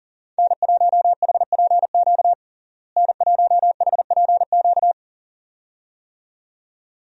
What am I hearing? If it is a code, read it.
Morse code, heard as a keyed tone, about 37 wpm, 700 Hz. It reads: N1HPQ N1HPQ
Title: USA Call Signs - 5 Characters - Repeated 2x - 40wpm